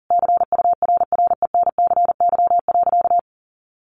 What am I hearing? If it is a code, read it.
Morse code, heard as a keyed tone, about 40 wpm, 700 Hz. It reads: CURRENCY.
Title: Sentences from Top 1500 Words 40wpm